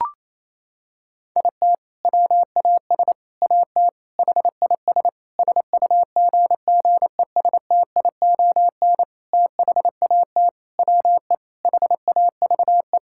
Morse code, heard as a keyed tone, about 28 wpm, 700 Hz. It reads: IT WAS AT HIS SUGGESTION THAT WE HAVE
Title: Sentences from Top 1600 Words 28wpm